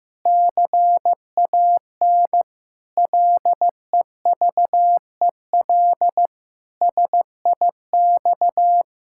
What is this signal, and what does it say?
Morse code, heard as a keyed tone, about 15 wpm, 700 Hz. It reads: CAN LEVEL SIX
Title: Pangram Challenge v2 - 15wpm